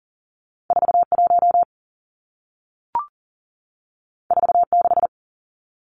Morse code, heard as a keyed tone, about 40 wpm, 700 Hz. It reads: 41 46
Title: Numbers - 2-Digit 40wpm